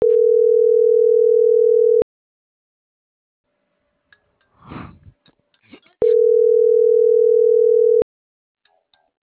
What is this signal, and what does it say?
Ringback tone (North America), 440 + 480 Hz.